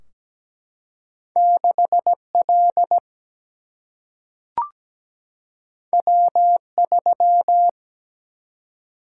Morse code, heard as a keyed tone, about 17 wpm, 700 Hz. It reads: 6L W3
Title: Single-Letter Single-Number 17wpm